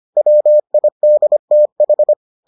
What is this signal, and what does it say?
Morse code, heard as a keyed tone, about 25 wpm, 600 Hz. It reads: WIDTH